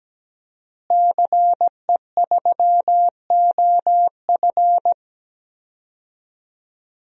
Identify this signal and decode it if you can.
Morse code, heard as a keyed tone, about 17 wpm, 700 Hz. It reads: CE3OF
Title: DXCC-100 Call Signs - 5 Characters - Repeated 2x - 17wpm